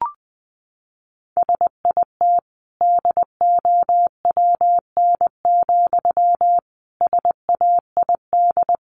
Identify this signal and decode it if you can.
Morse code, heard as a keyed tone, about 20 wpm, 700 Hz. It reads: SIT DOWN, SAID
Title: Sentences from Top 500 Words 20wpm